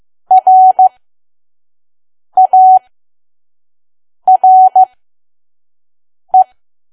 Morse code, heard as a keyed tone, about 15 wpm, spaced as for 5 wpm, 750 Hz. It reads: RARE
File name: Archive / 5 WPM files / PM